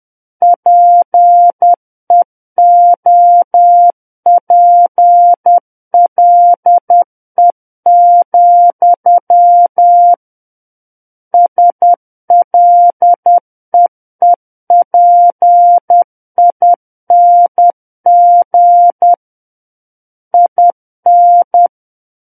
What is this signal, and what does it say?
Morse code, heard as a keyed tone, about 10 wpm, 700 Hz. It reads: PEOPLE, SLEEPING IN
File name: Book One / 01-Chapter 1 - 10 WPM